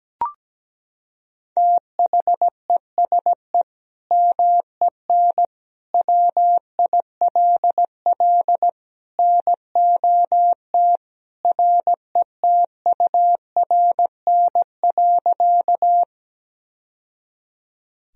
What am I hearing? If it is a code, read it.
Morse code, heard as a keyed tone, about 17 wpm, 700 Hz. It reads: THESE MEN WILL NOT RETURN.
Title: Sentences from Top 1000 Words - Review Part 1 - 17wpm